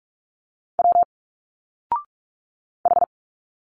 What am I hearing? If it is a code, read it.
Morse code, heard as a keyed tone, about 46 wpm, 700 Hz. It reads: W H
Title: Single Letters 45wpm